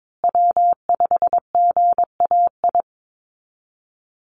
Morse code, heard as a keyed tone, about 22 wpm, 700 Hz. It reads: W5GAI
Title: Call Signs Repeated 2x - 22wpm